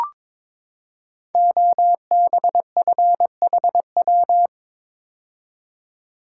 Morse code, heard as a keyed tone, about 22 wpm, 690 Hz. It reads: OBFHW
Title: Code Groups - 5 Letters - 22wpm